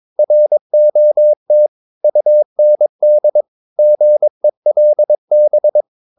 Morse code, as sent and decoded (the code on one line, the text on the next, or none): .-. --- - / ..- -. -.. / --. .-.. -...
ROT UND GELB